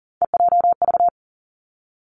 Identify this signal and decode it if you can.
Morse code, heard as a keyed tone, about 38 wpm, 700 Hz. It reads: EJV
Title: Three Challenging Letter-Number Combinations 40wpm